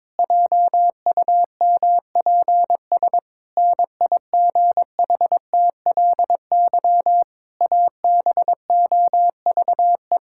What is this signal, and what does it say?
Morse code, heard as a keyed tone, about 22 wpm, 700 Hz. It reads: JUMPS NIGHTLY ABOVE
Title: Pangram Challenge v2 - 22wpm